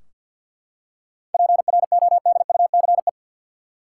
Morse code, heard as a keyed tone, about 45 wpm, 700 Hz. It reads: PRODUCE
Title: Seven Letter Words - 50wpm